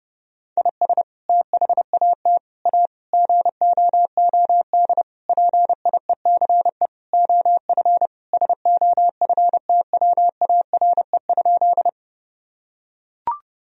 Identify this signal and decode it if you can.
Morse code, heard as a keyed tone, about 32 wpm, 700 Hz. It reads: IS THAT A GOOD PIECE OF SOFTWARE?
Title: Sentences from Top 1000 Words - Review Part 3 - 30wpm